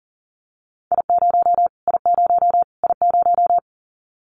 Morse code, heard as a keyed tone, about 40 wpm, 700 Hz. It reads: I0 I0 I0
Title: Callsign Prefixes for Top 25 DXCC - Repeated 3x - 40wpm